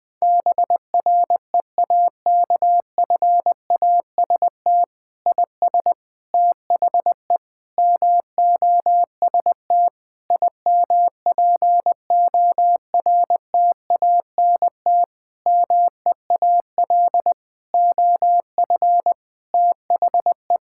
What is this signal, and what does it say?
Morse code, heard as a keyed tone, about 20 wpm, 700 Hz. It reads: BREAKFAST IS THE MOST IMPORTANT MEAL OF THE
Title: Difficult Sentences with Hints - 20wpm